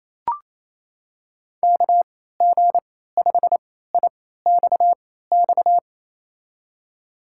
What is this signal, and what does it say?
Morse code, heard as a keyed tone, about 28 wpm, spaced as for 16 wpm, 700 Hz. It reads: KG5IXX
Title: USA Call Signs - 6 Characters - 3x Character Spacing - 28wpm